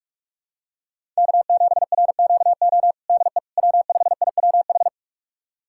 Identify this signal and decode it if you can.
Morse code, heard as a keyed tone, about 41 wpm, 700 Hz. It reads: K7RQO DE W5IWH
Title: USA Callsign DE Callsign - 5 Characters - 45wpm